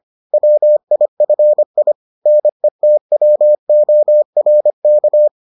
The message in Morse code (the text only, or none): WIFI NETWORK